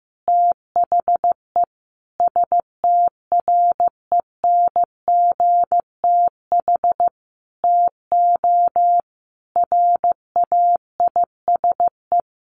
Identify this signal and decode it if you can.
Morse code, heard as a keyed tone, about 14 wpm, 700 Hz. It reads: THE STRENGTH TO RAISE
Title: Sentences from Top 1400 Words 15wpm